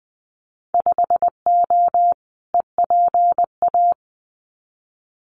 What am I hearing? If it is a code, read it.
Morse code, heard as a keyed tone, about 20 wpm, 700 Hz. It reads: 5O EPA
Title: Winter Field Day Exchanges Encore 20wpm